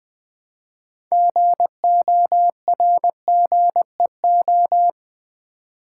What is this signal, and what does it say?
Morse code, heard as a keyed tone, about 20 wpm, 700 Hz. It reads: GORGEO
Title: Build A Word - 3 Letters At A Time - 8-Letter Words - 20wpm